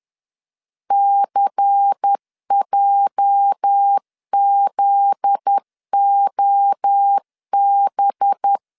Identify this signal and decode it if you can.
Morse code, heard as a keyed tone, about 10 wpm, 800 Hz. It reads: CJZOB
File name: LEZIONI / 8 YJ45 / 8 YJ 10wpm 2651